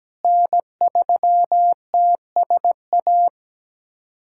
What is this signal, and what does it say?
Morse code, heard as a keyed tone, about 17 wpm, 700 Hz. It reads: N3TSA
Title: USA Callsign DE Callsign - 5 Characters - 17wpm